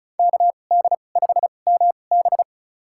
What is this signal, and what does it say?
Morse code, heard as a keyed tone, about 36 wpm, spaced as for 25 wpm, 700 Hz. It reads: KD5MB
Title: USA Call Signs - 5 Characters - 2x Character Spacing - 35wpm